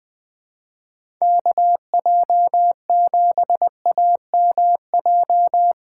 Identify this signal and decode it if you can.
Morse code, heard as a keyed tone, about 20 wpm, 700 Hz. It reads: KJ7AMJ
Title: USA Call Signs - 3x Character Spacing - 20wpm